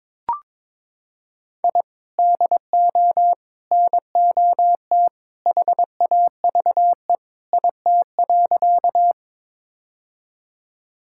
Morse code, heard as a keyed tone, about 22 wpm, 700 Hz. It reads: I DO NOT HAVE IT.
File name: practice / 22wpm / Sentences with 3-Word Phrases - 22wpm